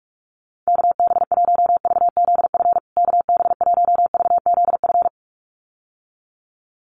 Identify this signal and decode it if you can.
Morse code, heard as a keyed tone, about 44 wpm, 700 Hz. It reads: KB1VZF KB1VZF
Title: USA Call Signs Repeated 2x 45wpm